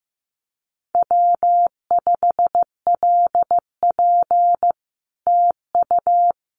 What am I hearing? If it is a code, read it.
Morse code, heard as a keyed tone, about 14 wpm, 700 Hz. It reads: W5LP TU
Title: Common QSO Phrases Encore 15wpm